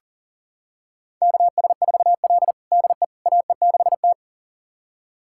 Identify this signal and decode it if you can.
Morse code, heard as a keyed tone, about 40 wpm, 700 Hz. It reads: KS4L DE AE6T
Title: USA Callsign DE Callsign - 4 Characters - 40wpm